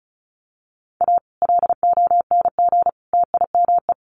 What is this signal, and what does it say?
Morse code, heard as a keyed tone, about 34 wpm, 700 Hz. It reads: A LONG TIME
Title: Sets of 3 Words 35wpm